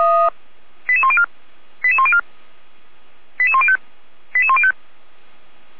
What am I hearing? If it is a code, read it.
Five-tone selective call (ZVEI), heard as quick runs of five tones, tones of about 70 ms. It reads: 88104 88104 88105 88105